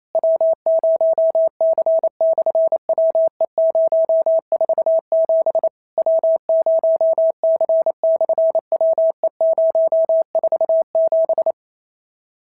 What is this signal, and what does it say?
Morse code, heard as a keyed tone, about 28 wpm, 650 Hz. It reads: W0C/WE047 W0C/WE047
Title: SOTA QSO Practice 28wpm